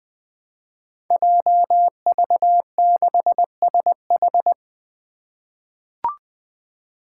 Morse code, heard as a keyed tone, about 20 wpm, 700 Hz. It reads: JV6SH
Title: Five Challenging Letter-Number Combinations 20wpm